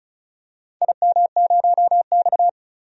Morse code, heard as a keyed tone, about 36 wpm, 700 Hz. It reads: IM0X
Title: DXCC-100 Call Signs - 4 Characters - 35wpm